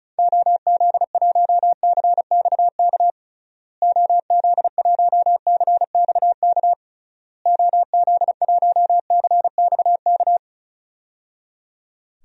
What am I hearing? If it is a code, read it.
Morse code, heard as a keyed tone, about 36 wpm, 700 Hz. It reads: OZ1CXK OZ1CXK OZ1CXK
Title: DXCC-100 Call Signs - 6 Characters - Repeated 3x - 35wpm